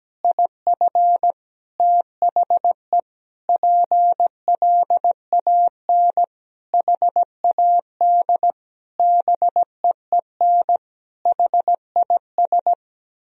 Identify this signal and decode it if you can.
Morse code, heard as a keyed tone, about 17 wpm, 700 Hz. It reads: IF THE PLAN HAD BEEN HIS
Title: Sentences from Top 1000 Words - Review Part 2 - 17wpm